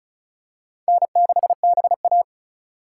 Morse code, heard as a keyed tone, about 36 wpm, 700 Hz. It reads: N6BA
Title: USA Call Signs - 4 Characters - 3x Character Spacing - 35wpm